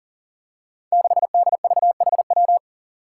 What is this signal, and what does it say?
Morse code, heard as a keyed tone, about 40 wpm, 700 Hz. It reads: 6DVHW
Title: Five Challenging Letter-Number Combinations 40wpm